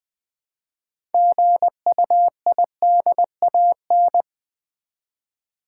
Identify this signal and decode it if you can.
Morse code, heard as a keyed tone, about 20 wpm, 690 Hz. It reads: GUIDAN
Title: Build A Word - 3 Letters At A Time - 8-Letter Words - 20wpm